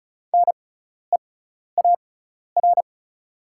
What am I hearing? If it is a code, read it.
Morse code, heard as a keyed tone, about 36 wpm, spaced as for 12 wpm, 700 Hz. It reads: NEAR
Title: 6x Character Spacing - Top 200 Words - 35wpm